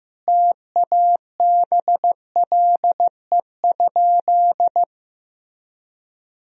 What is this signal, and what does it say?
Morse code, heard as a keyed tone, about 15 wpm, 700 Hz. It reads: TABLE?
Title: Sentences from Top 1000 Words - Review Part 3 - 15wpm